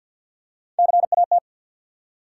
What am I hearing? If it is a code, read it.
Morse code, heard as a keyed tone, about 53 wpm, 700 Hz. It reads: CAT
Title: Three Letter Words - 50wpm